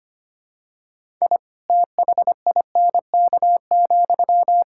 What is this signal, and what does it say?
Morse code, heard as a keyed tone, about 25 wpm, 700 Hz. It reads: I THINK,
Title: Sentences from Top 1000 Words - Review Part 5 - 25wpm